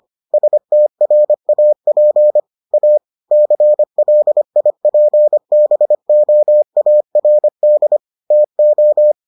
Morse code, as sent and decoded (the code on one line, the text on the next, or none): ... - .-. .- .--. / .- / -.-. .-.. .. .--. -... --- .- .-. -.. / - ---
STRAP A CLIPBOARD TO